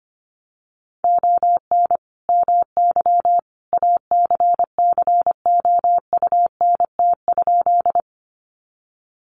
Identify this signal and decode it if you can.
Morse code, heard as a keyed tone, about 25 wpm, 700 Hz. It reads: ON MY ACCOUNT?
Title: Sentences from Top 900 Words 25wpm